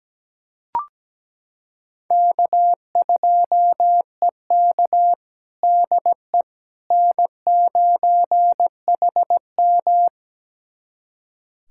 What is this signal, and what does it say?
Morse code, heard as a keyed tone, about 17 wpm, 700 Hz. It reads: K2EK DE N9HM
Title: USA Callsign DE Callsign - 4 Characters - 17wpm